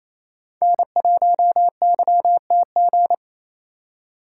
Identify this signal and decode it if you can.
Morse code, heard as a keyed tone, about 28 wpm, 700 Hz. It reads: N1YTG
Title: USA Call Signs - 5 Characters - 4x Character Spacing - 28wpm